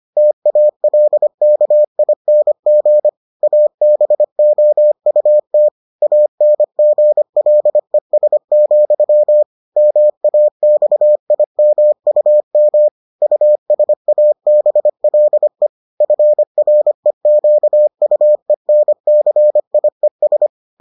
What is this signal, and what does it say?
Morse code, heard as a keyed tone, about 25 wpm, 600 Hz. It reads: TALKING ABOUT ANGLES, MAXIMUM USABLE FREQUENCIES